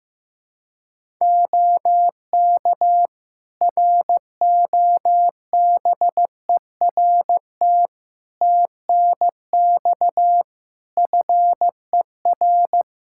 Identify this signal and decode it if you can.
Morse code, heard as a keyed tone, about 15 wpm, 700 Hz. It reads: OK ROBERT TNX FER
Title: Practice QSO - Exchange 4 - Ending - 15wpm